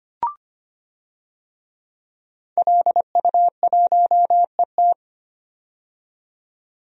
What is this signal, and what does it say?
Morse code, heard as a keyed tone, about 25 wpm, 700 Hz. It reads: LU1ET
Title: DXCC-100 Call Signs - 5 Characters - 25wpm